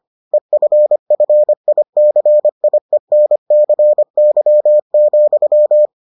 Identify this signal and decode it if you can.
Morse code, heard as a keyed tone, about 25 wpm, 600 Hz. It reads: EFFICIENCY,